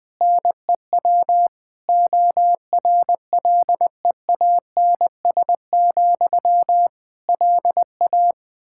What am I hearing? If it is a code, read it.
Morse code, heard as a keyed tone, about 20 wpm, 700 Hz. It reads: NEW ORLEANS, LA
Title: Practice QSO - Exchange 1 - Beginning - 20wpm